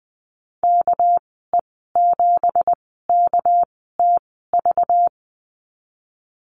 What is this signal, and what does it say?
Morse code, heard as a keyed tone, about 20 wpm, spaced as for 14 wpm, 700 Hz. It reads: KE7KTV
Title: 238 Lesson 38g - Letter q - Call Signs 20wpm